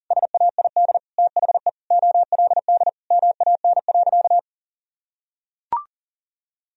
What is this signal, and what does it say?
Morse code, heard as a keyed tone, about 40 wpm, 700 Hz. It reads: SAID THE OLD MAN.
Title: Sentences from Top 1000 Words - Review Part 2 - 40wpm